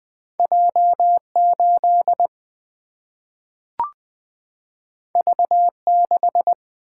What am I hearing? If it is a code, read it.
Morse code, heard as a keyed tone, about 20 wpm, 700 Hz. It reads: J8 V6
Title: Single-Letter Single-Number 20wpm